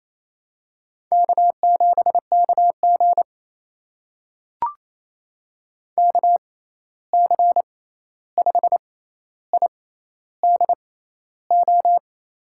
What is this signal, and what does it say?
Morse code, heard as a keyed tone, about 28 wpm, spaced as for 13 wpm, 700 Hz. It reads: K7KG KC5IDO
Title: USA Call Signs - 6x Character Spacing - 28wpm